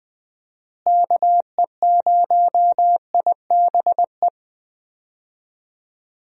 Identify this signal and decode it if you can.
Morse code, heard as a keyed tone, about 20 wpm, 700 Hz. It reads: KE0IBE
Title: USA Call Signs - 6 Characters - Repeated 3x - 20wpm